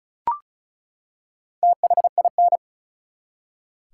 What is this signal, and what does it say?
Morse code, heard as a keyed tone, about 34 wpm, 700 Hz. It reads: THIN
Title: Four Letter Words - 35wpm